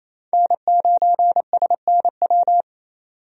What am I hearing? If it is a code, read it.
Morse code, heard as a keyed tone, about 28 wpm, 700 Hz. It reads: N9SNW
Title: USA Call Signs - 5 Characters - 28wpm